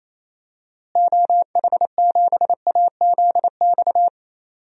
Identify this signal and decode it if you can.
Morse code, heard as a keyed tone, about 28 wpm, 700 Hz. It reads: OH7AZX
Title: DXCC-100 Call Signs - 6 Characters - Repeated 2x - 28wpm